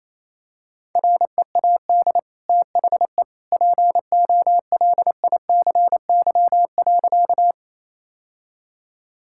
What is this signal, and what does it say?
Morse code, heard as a keyed tone, about 27 wpm, 700 Hz. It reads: READ THE POLICY.
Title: Sentences from Top 1000 Words - Review Part 5 - 28wpm